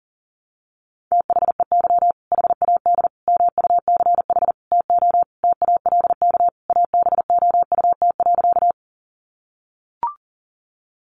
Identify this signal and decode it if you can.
Morse code, heard as a keyed tone, about 40 wpm, 700 Hz. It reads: THEY HAD MUCH TO TALK ABOUT.
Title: Sentences from Top 400 Words 40wpm